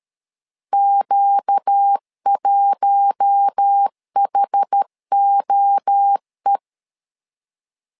Morse code, heard as a keyed tone, about 13 wpm, 800 Hz. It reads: Q1HOE
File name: LEZIONI / 4 Q1 12wpm 2028